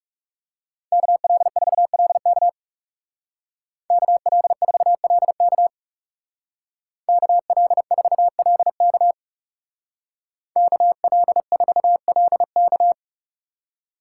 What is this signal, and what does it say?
Morse code, heard as a keyed tone, about 38 wpm, 700 Hz. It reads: KL4LK KL4LK KL4LK KL4LK